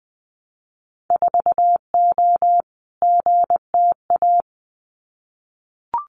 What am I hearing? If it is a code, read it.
Morse code, heard as a keyed tone, about 20 wpm, 700 Hz. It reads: 4O GTA
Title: Winter Field Day Exchanges 20wpm